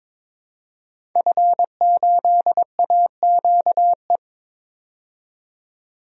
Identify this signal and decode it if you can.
Morse code, heard as a keyed tone, about 22 wpm, 700 Hz. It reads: F8AQE
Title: DXCC-100 Call Signs - 5 Characters - 22wpm